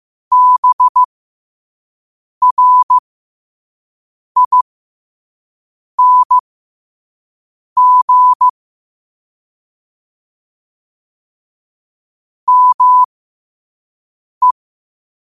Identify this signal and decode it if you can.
Morse code, heard as a keyed tone, about 15 wpm, spaced as for 5 wpm, 1000 Hz. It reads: BRING ME
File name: qotdfiles / 5wpm